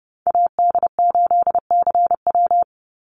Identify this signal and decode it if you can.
Morse code, heard as a keyed tone, about 30 wpm, 700 Hz. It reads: AD8CW
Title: Call Signs - 30wpm